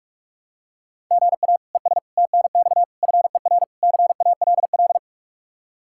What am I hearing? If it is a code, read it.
Morse code, heard as a keyed tone, about 46 wpm, 700 Hz. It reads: GA ES TNX FER CALL
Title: Practice QSO - Exchange 1 - Beginning - 45wpm